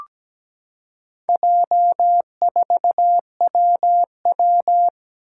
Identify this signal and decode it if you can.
Morse code, heard as a keyed tone, about 17 wpm, 700 Hz. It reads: J4WW
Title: Four Challenging Letter-Number Combinations 17wpm